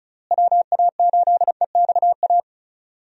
Morse code, heard as a keyed tone, about 36 wpm, 700 Hz. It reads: WA8EXA
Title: USA Call Signs - 6 Characters - Repeated 2x - 35wpm